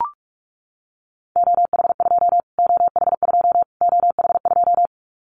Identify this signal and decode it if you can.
Morse code, heard as a keyed tone, about 44 wpm, 700 Hz. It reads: OH2 OH2 OH2
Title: Callsign Prefixes for Top 25 DXCC - Repeated 3x - 45wpm